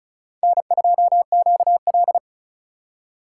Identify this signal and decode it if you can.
Morse code, heard as a keyed tone, about 36 wpm, 700 Hz. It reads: N2QL
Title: USA Call Signs - 4 Characters - 6x Character Spacing - 35wpm